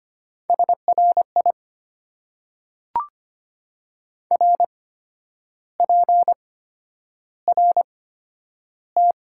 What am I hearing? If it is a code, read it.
Morse code, heard as a keyed tone, about 25 wpm, spaced as for 10 wpm, 700 Hz. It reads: SRI RPRT
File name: Top 90 QSO Elements - 8x Character Spacing - 25wpm